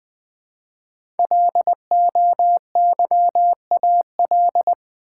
Morse code, heard as a keyed tone, about 20 wpm, 700 Hz. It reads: LOYAL
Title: 5 Letter Words - First 3 Letters Given - 20wpm